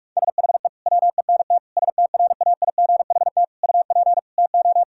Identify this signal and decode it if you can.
Morse code, heard as a keyed tone, about 46 wpm, 700 Hz. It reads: SHE WENT STRAIGHT UP TO
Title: Sentences from Top 1000 Words - Review Part 6 - 45wpm